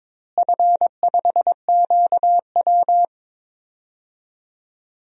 Morse code, heard as a keyed tone, about 22 wpm, 700 Hz. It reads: F5QW
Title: DXCC-100 Call Signs - 4 Characters - Repeated 3x - 22wpm